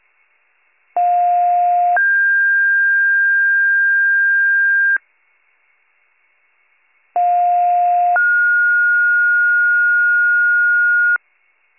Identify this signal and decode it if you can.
Two-tone page, heard as a one-second tone, then a three-second tone, about 710 then 1600 Hz, about 710 then 1430 Hz. It is heard from 1.0 to 11.2 s.